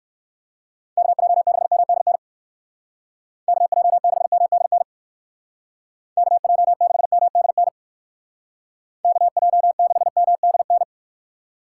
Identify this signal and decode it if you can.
Morse code, heard as a keyed tone, about 59 wpm, 700 Hz. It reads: KJ6MDN KJ6MDN KJ6MDN KJ6MDN